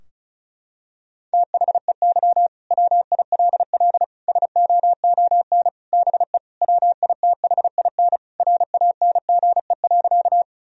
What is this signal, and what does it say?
Morse code, heard as a keyed tone, about 36 wpm, 700 Hz. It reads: THEY WILL SOON BE WITHIN RANGE.
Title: Sentences from Top 1000 Words - Review Part 4 - 35wpm